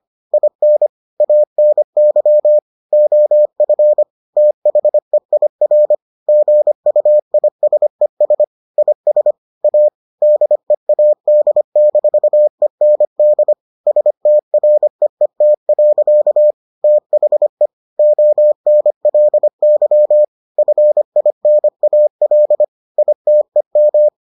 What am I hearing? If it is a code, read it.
Morse code, heard as a keyed tone, about 25 wpm, 600 Hz. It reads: IN ANY OF THEIR GUISES IS A DEAD-END STREET. THE ONLY FINAL ITEM